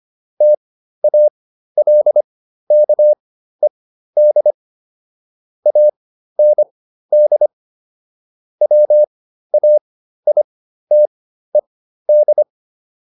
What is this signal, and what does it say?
Morse code, heard as a keyed tone, about 25 wpm, spaced as for 13 wpm, 600 Hz. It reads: TALKED AND WAITED